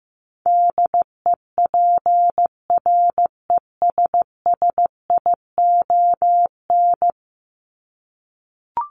Morse code, heard as a keyed tone, about 15 wpm, 700 Hz. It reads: DEPRESSION
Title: Top 1001-1250 Words 15wpm